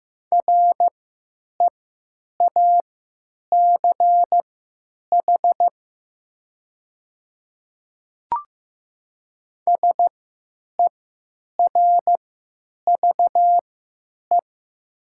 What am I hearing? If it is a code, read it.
Morse code, heard as a keyed tone, about 16 wpm, spaced as for 9 wpm, 700 Hz. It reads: REACH SERVE
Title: 3x Character Spacing - Top 500 Words - 15wpm